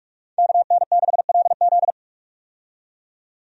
Morse code, heard as a keyed tone, about 44 wpm, 700 Hz. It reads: KN6LZ
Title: USA Call Signs - 5 Characters - 45wpm